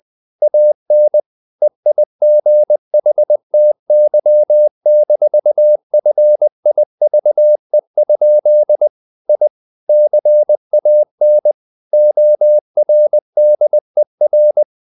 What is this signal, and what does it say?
Morse code, heard as a keyed tone, about 20 wpm, 600 Hz. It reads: AN EIGHTY-FIVE? I CAN ORDER
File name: the old man and the sea / the old man and the sea part 1 20wpm